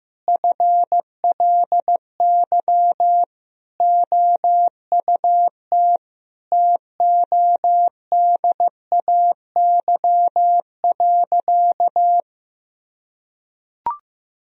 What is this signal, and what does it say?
Morse code, heard as a keyed tone, about 16 wpm, 700 Hz. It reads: FLY OUT TODAY.
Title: Sentences from Top 1000 Words - Review Part 2 - 15wpm